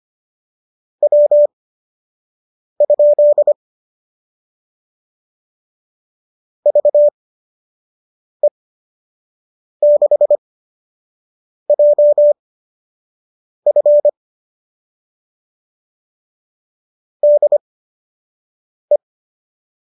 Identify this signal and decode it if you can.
Morse code, heard as a keyed tone, about 25 wpm, spaced as for 6 wpm, 570 Hz. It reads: W? VE6JF DE
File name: qso002 06